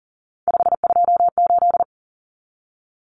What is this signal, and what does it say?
Morse code, heard as a keyed tone, about 40 wpm, 700 Hz. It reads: 528
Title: Common Cut Numbers - 3-Digit 40wpm